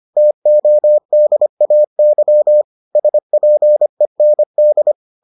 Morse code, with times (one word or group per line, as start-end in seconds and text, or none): 0.2-2.6 TODAY
3.0-4.9 SPEND